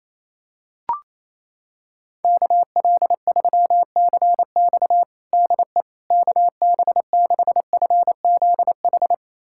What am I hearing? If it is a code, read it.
Morse code, heard as a keyed tone, about 28 wpm, 700 Hz. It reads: KL3CX DE KB6FZH
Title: USA Callsign DE Callsign 28wpm